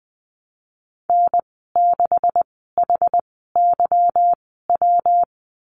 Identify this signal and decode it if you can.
Morse code, heard as a keyed tone, about 20 wpm, spaced as for 14 wpm, 700 Hz. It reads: N6HYW
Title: 161 Lesson 26g - Letter g - Call Signs 20wpm